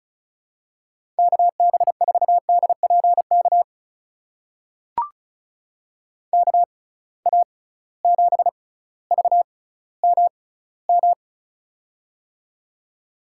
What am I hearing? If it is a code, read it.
Morse code, heard as a keyed tone, about 36 wpm, spaced as for 18 wpm, 700 Hz. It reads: KB4DPK KA7VMM
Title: USA Call Signs - 6x Character Spacing - 35wpm